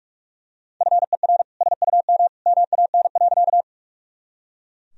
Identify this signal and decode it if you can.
Morse code, heard as a keyed tone, about 42 wpm, 700 Hz. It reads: FER SUM MAN.
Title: Sentences from Top 200 Words and Abbreviations 45wpm